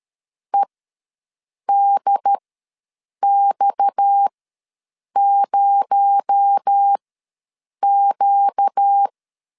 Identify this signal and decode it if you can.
Morse code, heard as a keyed tone, about 13 wpm, spaced as for 7 wpm, 800 Hz. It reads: EDX0Q
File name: LEZIONI / 10 SIGNS / 10B INT ws 12wpm 3527